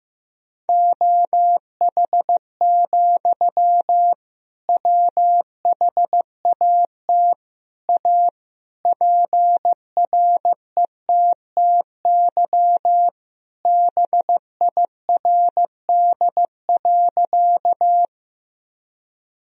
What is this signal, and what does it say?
Morse code, heard as a keyed tone, about 15 wpm, 700 Hz. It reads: OH, WHAT A PRETTY BIRD.
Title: Sentences from Top 1000 Words - Review Part 2 - 15wpm